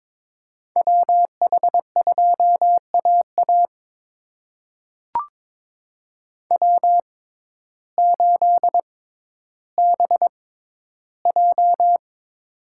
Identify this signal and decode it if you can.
Morse code, heard as a keyed tone, about 22 wpm, spaced as for 12 wpm, 700 Hz. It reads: WH2AA W8BJ
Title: USA Call Signs - 6x Character Spacing - 22wpm